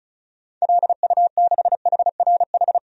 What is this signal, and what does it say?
Morse code, heard as a keyed tone, about 34 wpm, 700 Hz. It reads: LU6HRH
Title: DXCC-100 Call Signs - 6 Characters - 35wpm